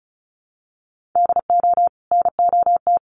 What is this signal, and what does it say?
Morse code, heard as a keyed tone, about 34 wpm, 700 Hz. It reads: DO NOT
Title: Sets of 3 Words 35wpm